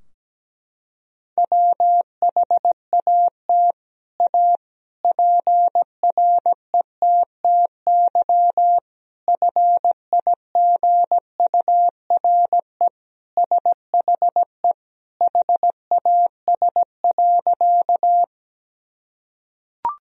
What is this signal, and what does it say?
Morse code, heard as a keyed tone, about 17 wpm, 700 Hz. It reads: WHAT A PRETTY FIGURE SHE HAS.
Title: Sentences from Top 1000 Words - Review Part 2 - 17wpm